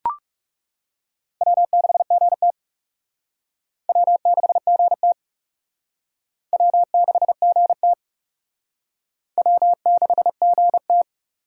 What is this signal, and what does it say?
Morse code, heard as a keyed tone, about 38 wpm, 700 Hz. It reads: W6GT W6GT W6GT W6GT